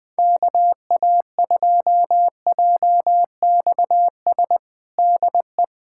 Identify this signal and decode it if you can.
Morse code, heard as a keyed tone, about 20 wpm, 700 Hz. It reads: KA2JXS DE
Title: USA Callsign DE Callsign - 6 Characters - 20wpm